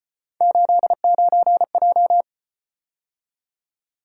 Morse code, heard as a keyed tone, about 33 wpm, 700 Hz. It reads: 89J